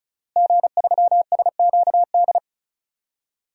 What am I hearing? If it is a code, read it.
Morse code, heard as a keyed tone, about 36 wpm, 700 Hz. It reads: G3SQD
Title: DXCC-100 Call Signs - 5 Characters - Repeated 3x - 35wpm